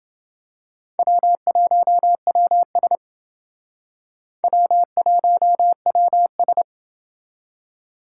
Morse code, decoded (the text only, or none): W1WS W1WS